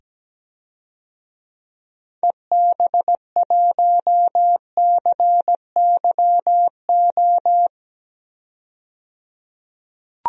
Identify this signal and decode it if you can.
Morse code, heard as a keyed tone, about 17 wpm, 700 Hz. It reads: EB1CYO
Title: DXCC-100 Call Signs - 6 Characters - 17wpm